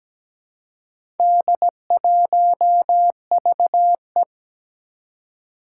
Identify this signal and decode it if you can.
Morse code, heard as a keyed tone, about 17 wpm, 700 Hz. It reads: D1VE
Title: Four Challenging Letter-Number Combinations 17wpm